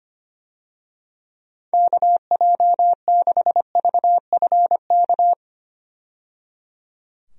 Morse code, heard as a keyed tone, about 25 wpm, 700 Hz. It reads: KJ6VFK
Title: USA Call Signs - 6 Characters - 25wpm